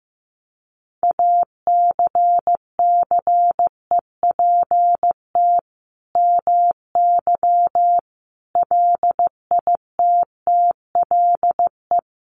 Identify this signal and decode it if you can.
Morse code, heard as a keyed tone, about 15 wpm, 700 Hz. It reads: ACCEPT MY LITTLE
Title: Sentences from Top 1300 Words 15wpm